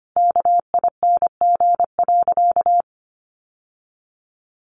Morse code, heard as a keyed tone, about 25 wpm, 700 Hz. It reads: KING.
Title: Sentences from Top 1200 Words 25wpm